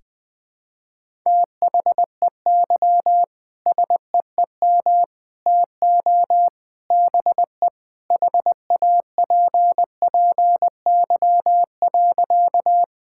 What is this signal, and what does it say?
Morse code, heard as a keyed tone, about 21 wpm, 700 Hz. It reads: THEY SEEM TO BE HAPPY.
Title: Sentences with 3-Word Phrases - 20wpm